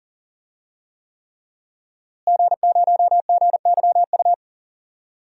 Morse code, heard as a keyed tone, about 39 wpm, 700 Hz. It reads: G0GYU
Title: DXCC-100 Call Signs - 5 Characters - Repeated 3x - 40wpm